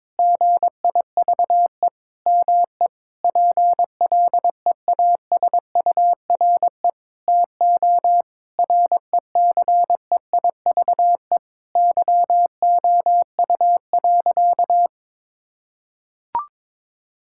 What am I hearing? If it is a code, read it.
Morse code, heard as a keyed tone, about 22 wpm, 700 Hz. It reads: GIVE ME PLEASURE TO RECEIVE YOU.
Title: Sentences from Top 1600 Words 22wpm